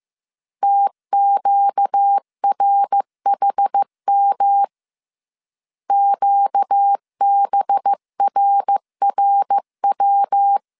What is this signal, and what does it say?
Morse code, heard as a keyed tone, about 15 wpm, 800 Hz. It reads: TQRHM QBRRW